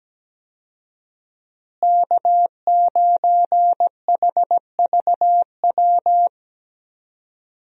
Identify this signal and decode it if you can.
Morse code, heard as a keyed tone, about 17 wpm, 700 Hz. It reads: K9HVW
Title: USA Call Signs - 5 Characters - Repeated 2x - 17wpm